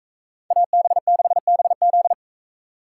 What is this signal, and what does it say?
Morse code, heard as a keyed tone, about 44 wpm, 700 Hz. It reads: AB6BZ